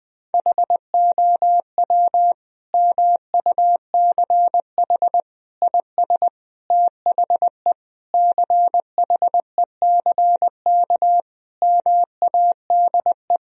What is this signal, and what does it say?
Morse code, heard as a keyed tone, about 20 wpm, 700 Hz. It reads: HOW MUCH IS THE CHECK MADE
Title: Sentences from Top 1000 Words - Review Part 3 - 20wpm